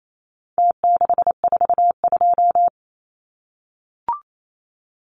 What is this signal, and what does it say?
Morse code, heard as a keyed tone, about 28 wpm, 700 Hz. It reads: T642
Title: Numbers - Contest Style - 4-Digit 28wpm